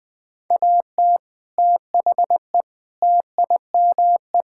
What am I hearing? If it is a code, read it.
Morse code, heard as a keyed tone, about 20 wpm, 700 Hz. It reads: AT THE TIME
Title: Sentences with 4-Word Phrases - Spoken Beginning - 20wpm